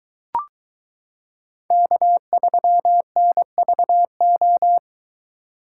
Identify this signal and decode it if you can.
Morse code, heard as a keyed tone, about 22 wpm, 700 Hz. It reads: K3NVO